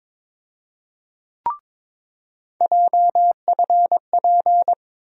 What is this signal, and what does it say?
Morse code, heard as a keyed tone, about 22 wpm, 700 Hz. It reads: JFP
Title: USA Call Sign Postfixes - 3 Letters 22wpm